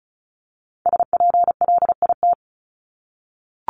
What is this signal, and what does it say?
Morse code, heard as a keyed tone, about 34 wpm, 700 Hz. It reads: SPLIT
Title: Top 1751-2000 Words 35wpm